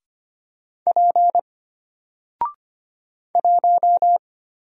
Morse code, heard as a keyed tone, about 25 wpm, 700 Hz. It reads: P 1
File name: Single-Letter-Number-25wpm